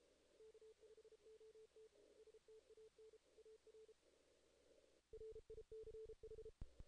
Morse code, heard as a keyed tone, about 32 wpm, 440 Hz. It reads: K5OT STAN AR RICH
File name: 2016 JAN CW NAQP / 2016 Jan naqp k5ot - 0-9